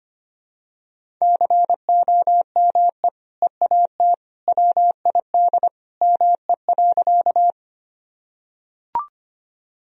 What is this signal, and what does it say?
Morse code, heard as a keyed tone, about 25 wpm, 700 Hz. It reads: COME EAT WID ME.
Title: Sentences from Top 300 Words and Abbreviations 25wpm